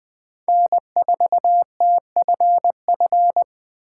Morse code, heard as a keyed tone, about 20 wpm, 700 Hz. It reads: N4TFF
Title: USA Call Signs - 5 Characters - 3x Character Spacing - 20wpm